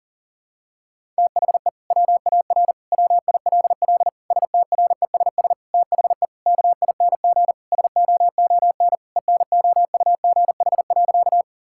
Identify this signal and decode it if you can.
Morse code, heard as a keyed tone, about 40 wpm, 700 Hz. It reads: THE WAR WILL STRESS THE KING SOON ENOUGH.
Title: Sentences from Top 1000 Words - Review Part 6 - 40wpm